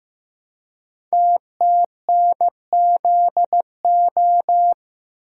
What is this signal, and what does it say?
Morse code, heard as a keyed tone, about 15 wpm, 700 Hz. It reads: TTNZO
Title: Code Groups - 5 Letters - 15wpm